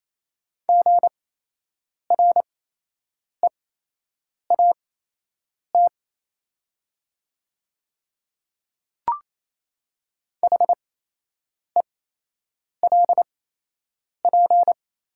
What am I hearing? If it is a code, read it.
Morse code, heard as a keyed tone, about 27 wpm, spaced as for 8 wpm, 700 Hz. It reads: GREAT HELP
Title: 8x Character Spacing - Top 300 Words - 28wpm